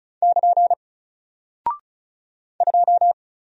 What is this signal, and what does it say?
Morse code, heard as a keyed tone, about 36 wpm, 700 Hz. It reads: <KN> 2
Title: Single Character (Letter:Number:Punctuation:Prosign) 35wpm